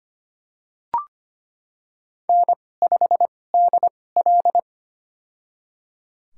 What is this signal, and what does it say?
Morse code, heard as a keyed tone, about 25 wpm, spaced as for 18 wpm, 700 Hz. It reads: N5DL
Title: USA Call Signs - 4 Characters - 2x Character Spacing - 25wpm